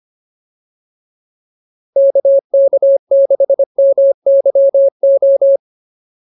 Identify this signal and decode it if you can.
Morse code, heard as a keyed tone, about 25 wpm, 550 Hz. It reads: KK6MYO